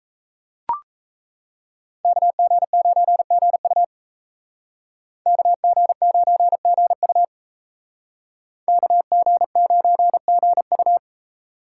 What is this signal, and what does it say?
Morse code, heard as a keyed tone, about 36 wpm, 700 Hz. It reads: KG9GU KG9GU KG9GU